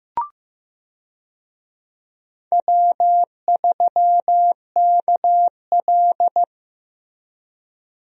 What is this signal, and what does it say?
Morse code, heard as a keyed tone, about 15 wpm, 700 Hz. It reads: W3KL
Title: USA Call Signs - 4 Characters - Repeated 3x - 15wpm